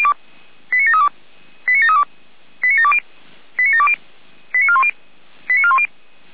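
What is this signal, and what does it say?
Five-tone selective call (ZVEI), heard as quick runs of five tones, tones of about 70 ms. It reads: 89732 89732 89720 89720 89410 89410
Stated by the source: ZVEI five-tone selective call received on an SDR.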